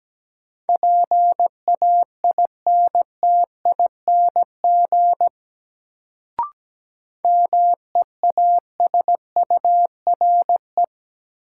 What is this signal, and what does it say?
Morse code, heard as a keyed tone, about 17 wpm, 700 Hz. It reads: PAINTING MEASURE
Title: Sentences from Top 1200 Words - New Word Drill Alt - 17wpm